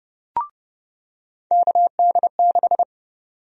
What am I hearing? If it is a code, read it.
Morse code, heard as a keyed tone, about 30 wpm, 700 Hz. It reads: KD6
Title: USA Call Sign Prefixes - 30wpm